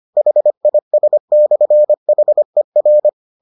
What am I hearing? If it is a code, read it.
Morse code, heard as a keyed tone, about 25 wpm, 600 Hz. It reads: HIS/HER